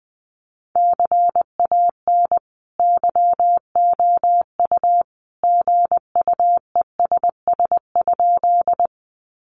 Morse code, heard as a keyed tone, about 20 wpm, 700 Hz. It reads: CAN YOU GUESS?
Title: Sentences from Top 1500 Words 20wpm